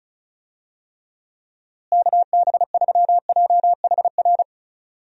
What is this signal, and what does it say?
Morse code, heard as a keyed tone, about 36 wpm, 700 Hz. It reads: KB3JHR
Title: USA Call Signs - 6 Characters - Repeated 2x - 35wpm